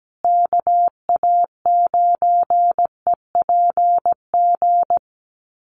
Morse code, heard as a keyed tone, about 17 wpm, 700 Hz. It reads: KA9EPG
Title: USA Call Signs Repeated 3x 17wpm